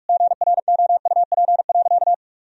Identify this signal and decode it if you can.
Morse code, heard as a keyed tone, about 41 wpm, 700 Hz. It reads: GROUP.
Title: Sentences from Top 1000 Words - Review Part 1 - 45wpm